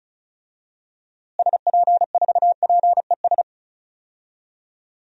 Morse code, heard as a keyed tone, about 34 wpm, 700 Hz. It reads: SP4PES
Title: DXCC-100 Call Signs - 6 Characters - Repeated 2x - 35wpm